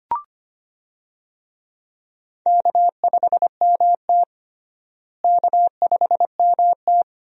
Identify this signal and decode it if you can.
Morse code, heard as a keyed tone, about 25 wpm, 700 Hz. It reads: K5MT K5MT
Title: USA Call Signs - 4 Characters - Repeated 2x - 25wpm